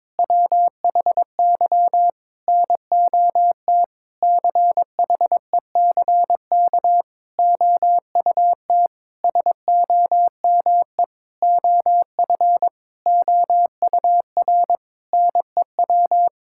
Morse code, heard as a keyed tone, about 21 wpm, 700 Hz. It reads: WHY NOT CHECK OUT SOME OF OUR NEW
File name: Sentences with 9-Word Phrases - 22wpm